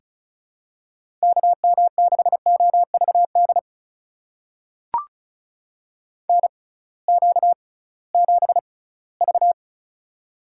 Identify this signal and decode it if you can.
Morse code, heard as a keyed tone, about 36 wpm, spaced as for 20 wpm, 700 Hz. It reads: KM6OVD NQ7V
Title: USA Call Signs - 6x Character Spacing - 35wpm